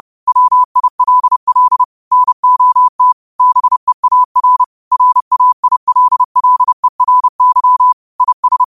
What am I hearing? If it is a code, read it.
Morse code, heard as a keyed tone, about 30 wpm, 1000 Hz. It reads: WILL NOT BEAR RAILLERY IS